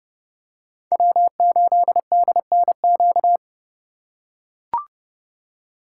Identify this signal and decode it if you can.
Morse code, heard as a keyed tone, about 28 wpm, 700 Hz. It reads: W8DNQ